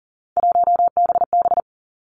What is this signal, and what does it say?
Morse code, heard as a keyed tone, about 38 wpm, 700 Hz. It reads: 1BB
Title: Three Challenging Letter-Number Combinations 40wpm